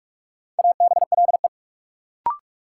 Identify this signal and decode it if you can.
Morse code, heard as a keyed tone, about 44 wpm, 700 Hz. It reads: ABLE
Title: Sets of 2 Words - New Word Drill - 45wpm